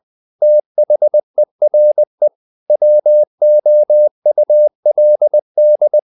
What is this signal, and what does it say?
Morse code, heard as a keyed tone, about 20 wpm, 600 Hz. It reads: THERE WOULD
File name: the old man and the sea part 2 20wpm